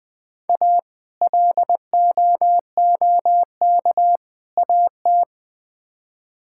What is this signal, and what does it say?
Morse code, heard as a keyed tone, about 20 wpm, 700 Hz. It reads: A LOOK AT
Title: Build A Phrase - 3-Word Phrases - 20wpm